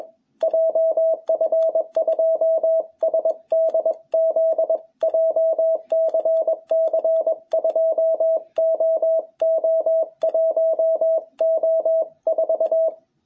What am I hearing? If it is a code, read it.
Morse code, heard as a keyed tone, about 21 wpm, 660 Hz. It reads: JF2SDZ JCC 2OO1O4
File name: jh8cbh / 2021cw